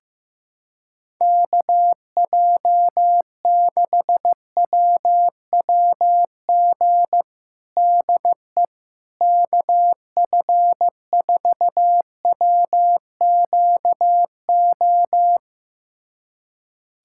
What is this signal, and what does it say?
Morse code, heard as a keyed tone, about 15 wpm, 700 Hz. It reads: KJ6WWG DE KF4WQO
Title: USA Callsign DE Callsign - 6 Characters - 15wpm